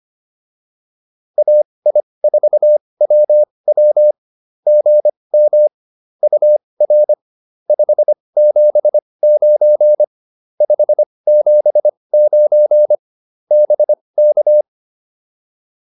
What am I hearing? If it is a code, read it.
Morse code, heard as a keyed tone, about 25 wpm, spaced as for 20 wpm, 600 Hz. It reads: AI4WW GM UR 579 579 BK